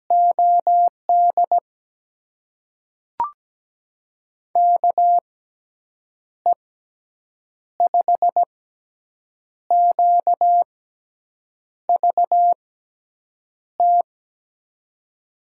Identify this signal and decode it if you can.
Morse code, heard as a keyed tone, about 17 wpm, spaced as for 7 wpm, 700 Hz. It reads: OD KE5QVT
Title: USA Call Signs - 6x Character Spacing - 17wpm